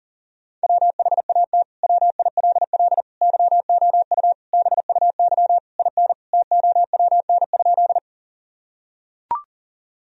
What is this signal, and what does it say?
Morse code, heard as a keyed tone, about 40 wpm, 700 Hz. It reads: WHAT WILL YOU BUY IN TOWN?
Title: Sentences from Top 1000 Words - Review Part 4 - 40wpm